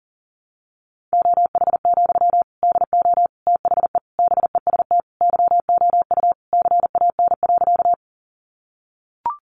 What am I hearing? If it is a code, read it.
Morse code, heard as a keyed tone, about 40 wpm, 700 Hz. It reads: OH, DO THE BEST YOU CAN.
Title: Sentences from Top 500 Words 40wpm